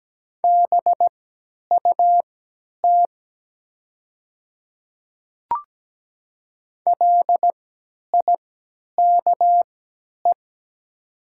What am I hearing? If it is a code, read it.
Morse code, heard as a keyed tone, about 17 wpm, spaced as for 10 wpm, 690 Hz. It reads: BUT LIKE
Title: 3x Character Spacing - Top 100 Words - 17wpm